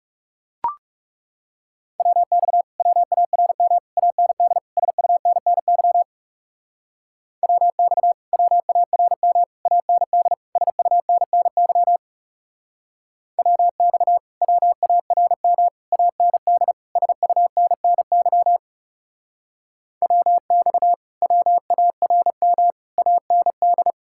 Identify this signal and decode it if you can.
Morse code, heard as a keyed tone, about 36 wpm, 700 Hz. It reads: WX WARM AND SUNNY WX WARM AND SUNNY WX WARM AND SUNNY WX WARM AND
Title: Speed-Racing Common QSO Phrases - 30wpm